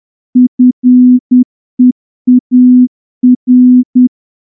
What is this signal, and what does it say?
Morse code, heard as a keyed tone, about 10 wpm, 250 Hz. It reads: FEAR
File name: morse-training-10WPM